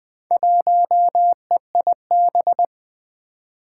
Morse code, heard as a keyed tone, about 20 wpm, 700 Hz. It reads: 1EIB
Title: Four Challenging Letter-Number Combinations 20wpm